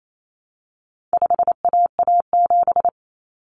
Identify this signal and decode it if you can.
Morse code, heard as a keyed tone, about 28 wpm, 700 Hz. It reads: 5AA7
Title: Common Cut Numbers - 4-Digit 28wpm